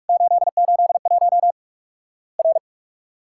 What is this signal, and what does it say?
Morse code, heard as a keyed tone, about 46 wpm, 700 Hz. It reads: 881 R
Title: Worked All Europe DX Contest - QTC Exchange - 45wpm